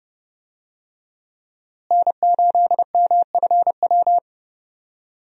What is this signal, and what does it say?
Morse code, heard as a keyed tone, about 30 wpm, 700 Hz. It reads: N8MFW
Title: USA Call Signs - 5 Characters - Repeated 3x - 30wpm